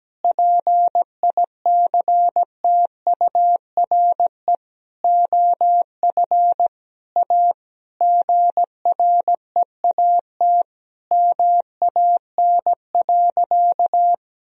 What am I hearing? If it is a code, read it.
Morse code, heard as a keyed tone, about 17 wpm, 700 Hz. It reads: PICTURE OF A GREAT MAN.
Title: Sentences from Top 1000 Words - Review Part 2 - 17wpm